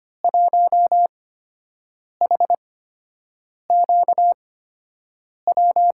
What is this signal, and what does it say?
Morse code, heard as a keyed tone, about 25 wpm, spaced as for 7 wpm, 700 Hz. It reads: 1HQW
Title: USA Call Signs - 8x Character Spacing - 25wpm